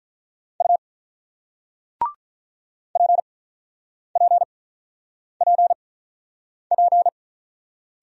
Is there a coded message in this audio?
Yes: Morse code from 0.6 to 7.1 s, heard as a keyed tone, about 48 wpm, spaced as for 9 wpm, 700 Hz. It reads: U PPPP